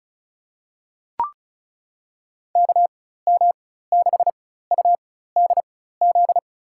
Morse code, heard as a keyed tone, about 36 wpm, spaced as for 16 wpm, 700 Hz. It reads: KM6UDZ
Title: USA Call Signs - 4x Character Spacing - 35wpm